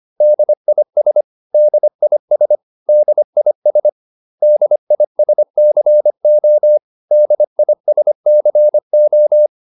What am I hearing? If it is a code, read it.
Morse code, heard as a keyed tone, about 25 wpm, 600 Hz. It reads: DIS DIS DIS DISCO DISCO